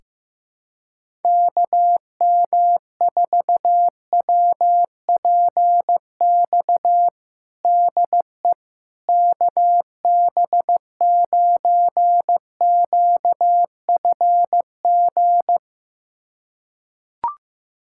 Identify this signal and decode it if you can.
Morse code, heard as a keyed tone, about 15 wpm, 700 Hz. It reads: KM4WPX DE KB9QFG
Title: USA Callsign DE Callsign - 6 Characters - 15wpm